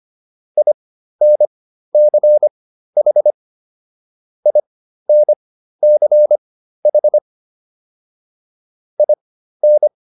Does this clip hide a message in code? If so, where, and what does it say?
Yes: Morse code from 0.6 to 9.9 s, heard as a keyed tone, about 25 wpm, spaced as for 13 wpm, 600 Hz. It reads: INCH INCH IN